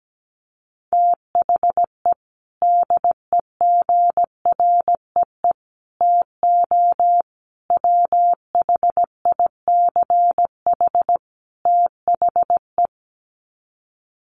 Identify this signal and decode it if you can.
Morse code, heard as a keyed tone, about 17 wpm, 700 Hz. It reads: THE DEGREE TO WHICH THE
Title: Sets of 5 Words Encore2 17wpm